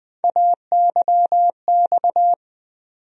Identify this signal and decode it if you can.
Morse code, heard as a keyed tone, about 20 wpm, 700 Hz. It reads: AYX
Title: USA Call Sign Postfixes - 3 Letters 20wpm